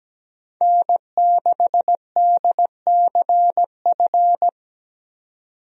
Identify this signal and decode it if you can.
Morse code, heard as a keyed tone, about 17 wpm, 700 Hz. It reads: N6DCF
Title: USA Call Signs - 5 Characters - 17wpm